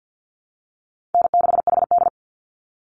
Morse code, heard as a keyed tone, about 48 wpm, 700 Hz. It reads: N6HD
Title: Easy Call Signs Repeated 3x - 50wpm